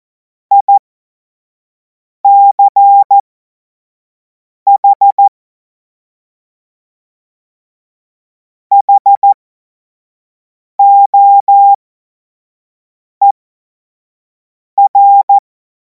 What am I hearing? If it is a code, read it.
Morse code, heard as a keyed tone, about 14 wpm, spaced as for 5 wpm, 800 Hz. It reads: ICH HOER
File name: Morsekurs / Witze / Witz 17. März 2026 25 F70